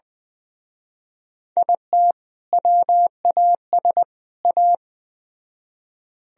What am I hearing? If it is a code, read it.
Morse code, heard as a keyed tone, about 20 wpm, 700 Hz. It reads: IT WAS A